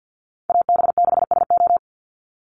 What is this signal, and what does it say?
Morse code, heard as a keyed tone, about 50 wpm, 700 Hz. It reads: AB6SO
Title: NA SOTA Chaser Callsigns 50wpm